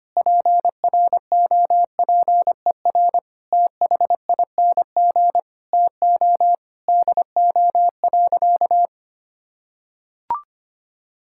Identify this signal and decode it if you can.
Morse code, heard as a keyed tone, about 25 wpm, 700 Hz. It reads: PROPER THING TO DO.
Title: Sentences from Top 1000 Words - Review Part 5 - 25wpm